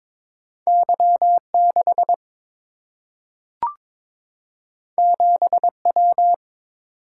Morse code, heard as a keyed tone, about 23 wpm, 700 Hz. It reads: Y 6 7 W
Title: Single-Letter Single-Number 22wpm